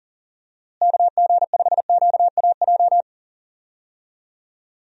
Morse code, heard as a keyed tone, about 40 wpm, 700 Hz. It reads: KG5QAJ
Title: USA Call Signs - 6 Characters - Repeated 2x - 40wpm